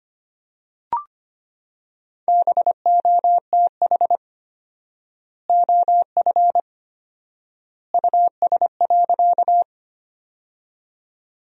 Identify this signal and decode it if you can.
Morse code, heard as a keyed tone, about 25 wpm, 700 Hz. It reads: BOTH OF US.
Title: 4x Word-Spacing - Sentences from Top 300 Words - 25wpm